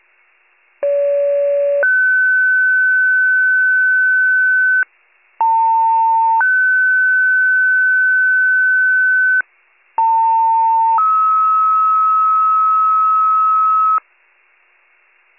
Two-tone page, heard as a one-second tone, then a three-second tone, about 570 then 1500 Hz, about 900 then 1500 Hz, about 900 then 1290 Hz. It is heard from 0.8 to 14.0 s.